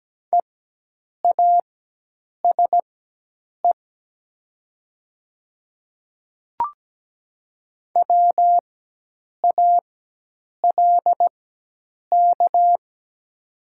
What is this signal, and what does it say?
Morse code, heard as a keyed tone, about 17 wpm, spaced as for 8 wpm, 700 Hz. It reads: EASE WALK
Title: 4x Character Spacing - Top 300 Words - 17wpm